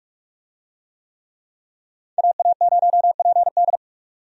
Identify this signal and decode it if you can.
Morse code, heard as a keyed tone, about 42 wpm, 700 Hz. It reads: AA0PD
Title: USA Call Signs - 5 Characters - 45wpm